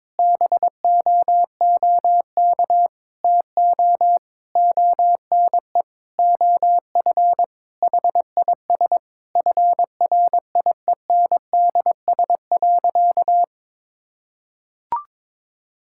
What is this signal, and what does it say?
Morse code, heard as a keyed tone, about 22 wpm, 700 Hz. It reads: BOOK TO ONE OF HIS FRIENDS.
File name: practice / 22wpm / Sentences with 4-Word Phrases - 22wpm